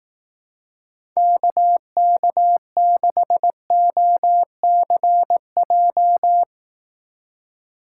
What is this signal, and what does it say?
Morse code, heard as a keyed tone, about 18 wpm, 700 Hz. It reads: KK6OCJ
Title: Speed Racing - USA Call Signs 15wpm